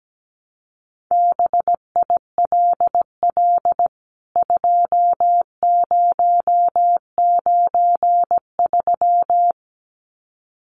Morse code, heard as a keyed tone, about 17 wpm, 700 Hz. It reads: BILL 2093
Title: CWT Exchange 17wpm